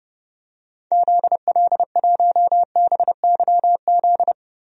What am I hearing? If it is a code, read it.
Morse code, heard as a keyed tone, about 30 wpm, 700 Hz. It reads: ZL1BYZ
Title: DXCC-100 Call Signs - 6 Characters - Repeated 2x - 30wpm